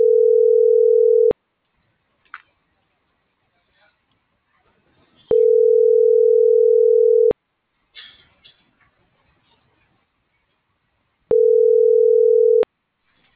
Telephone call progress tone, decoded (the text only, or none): ringback tone